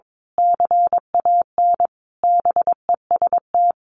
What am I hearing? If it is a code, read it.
Morse code, heard as a keyed tone, about 22 wpm, 700 Hz. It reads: CAN BEST
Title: Sentences from Top 1400 Words 22wpm